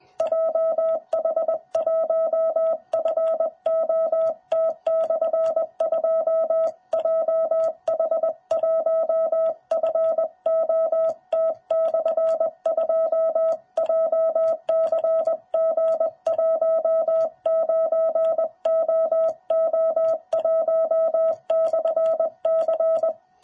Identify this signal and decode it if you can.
Morse code, heard as a keyed tone, about 20 wpm, 660 Hz. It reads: JH1FOT/2 JH1FOT/2 JCG19OO1/C
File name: jh8cbh / 2021cw